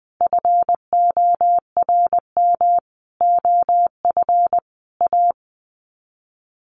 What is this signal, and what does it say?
Morse code, heard as a keyed tone, about 20 wpm, 700 Hz. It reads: FORM OF A
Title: Sets of 5 Words 20wpm